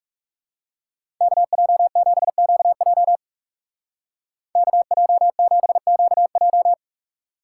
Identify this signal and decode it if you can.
Morse code, heard as a keyed tone, about 38 wpm, 700 Hz. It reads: KJ7QJ KJ7QJ
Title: Speed Racing - USA Call Signs 30wpm